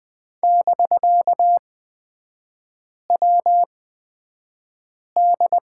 Morse code, heard as a keyed tone, about 20 wpm, spaced as for 5 wpm, 700 Hz. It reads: <BK>WD
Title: 250 Lesson 40e - BK - Three Characters 20wpm